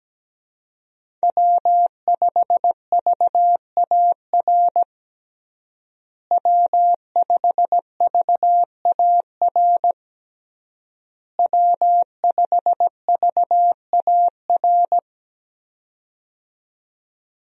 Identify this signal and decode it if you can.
Morse code, heard as a keyed tone, about 17 wpm, 700 Hz. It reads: W5VAR W5VAR W5VAR
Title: USA Call Signs - 5 Characters - Repeated 3x - 17wpm